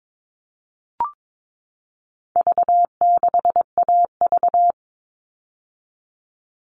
Morse code, heard as a keyed tone, about 22 wpm, 700 Hz. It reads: V6AV
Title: All Cut Numbers - 4-Digit 22wpm